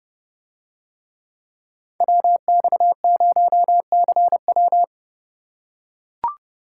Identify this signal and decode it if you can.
Morse code, heard as a keyed tone, about 30 wpm, 700 Hz. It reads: WX0CW